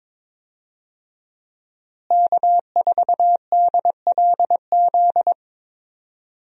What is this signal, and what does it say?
Morse code, heard as a keyed tone, about 22 wpm, 700 Hz. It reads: K4DLZ
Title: USA Call Signs - 5 Characters - Repeated 3x - 22wpm